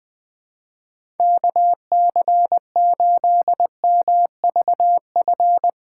Morse code, heard as a keyed tone, about 21 wpm, 700 Hz. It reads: KC8MVF
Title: USA Callsign DE Callsign 20wpm